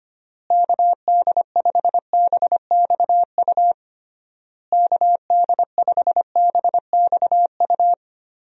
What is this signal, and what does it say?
Morse code, heard as a keyed tone, about 25 wpm, 700 Hz. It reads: KD5BXU KD5BXU
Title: USA Call Signs - 6 Characters - Repeated 3x - 25wpm